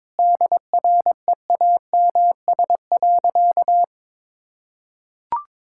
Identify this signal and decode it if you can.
Morse code, heard as a keyed tone, about 21 wpm, 700 Hz. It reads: DREAMS.
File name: Difficult Sentences with Hints - 22wpm